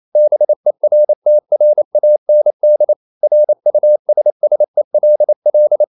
Morse code, heard as a keyed tone, about 28 wpm, 600 Hz. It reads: BERTRAND RUSSELL
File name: CW / Practice / EnglishQuotations0128